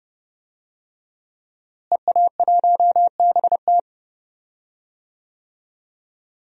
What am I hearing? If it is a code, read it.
Morse code, heard as a keyed tone, about 30 wpm, 700 Hz. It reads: EA1BT
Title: DXCC-100 Call Signs - 5 Characters - Repeated 2x - 30wpm